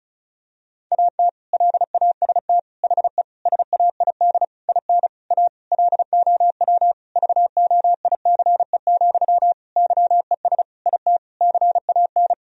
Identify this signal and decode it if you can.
Morse code, heard as a keyed tone, about 34 wpm, 700 Hz. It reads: AT LAST HE SAID IN A LOW VOICE, YES IT CAN
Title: Sentences from Top 1000 Words - Review Part 1 - 35wpm